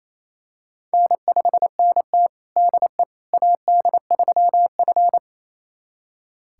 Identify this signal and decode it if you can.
Morse code, heard as a keyed tone, about 28 wpm, 700 Hz. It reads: N5NT DE AD3F
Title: USA Callsign DE Callsign - 4 Characters - 28wpm